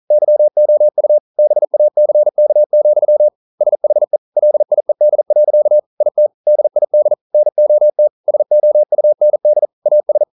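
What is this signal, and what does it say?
Morse code, heard as a keyed tone, about 39 wpm, 600 Hz. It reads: YOU BACK, SHE LIED. IT DID NOT SOUND AS
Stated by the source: Morse code practice at 41 wpm, from CW med Gnister.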